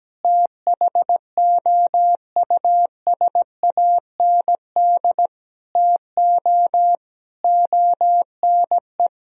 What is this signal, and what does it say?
Morse code, heard as a keyed tone, about 17 wpm, 700 Hz. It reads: THOUSAND TO ONE
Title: Sentences from Top 1000 Words - Review Part 3 - 17wpm